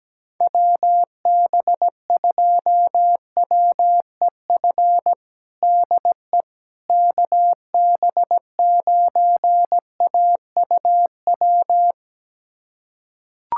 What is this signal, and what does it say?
Morse code, heard as a keyed tone, about 17 wpm, 700 Hz. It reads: WB2WEF DE KB9AUW
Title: USA Callsign DE Callsign 17wpm